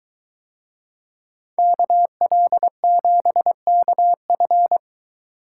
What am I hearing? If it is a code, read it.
Morse code, heard as a keyed tone, about 21 wpm, 700 Hz. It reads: KL7KF